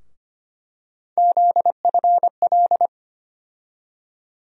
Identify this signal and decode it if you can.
Morse code, heard as a keyed tone, about 25 wpm, 700 Hz. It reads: ZFL
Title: USA Call Sign Postfixes - 3 Letters 25wpm